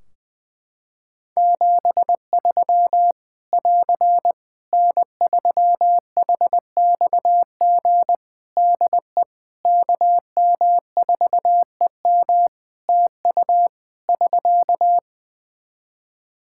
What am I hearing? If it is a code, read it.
Morse code, heard as a keyed tone, about 20 wpm, 700 Hz. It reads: 73 <AR> N3HXG DE KM4EM TU <SK>
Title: Practice QSO - Exchange 4 - Ending - 20wpm